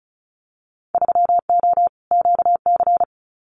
Morse code, heard as a keyed tone, about 36 wpm, 700 Hz. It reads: 3O QC
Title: Winter Field Day Exchanges Encore 35wpm